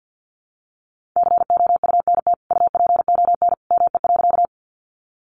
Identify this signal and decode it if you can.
Morse code, heard as a keyed tone, about 50 wpm, 700 Hz. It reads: COUNT UPON ME.
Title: Sentences from Top 1500 Words 50wpm